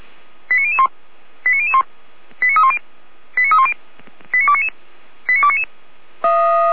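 Five-tone selective call (ZVEI), heard as quick runs of five tones, tones of about 70 ms. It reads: 89001 89001 89310 89310 89299 89299
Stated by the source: ZVEI five-tone selective call 89290 received on an SDR.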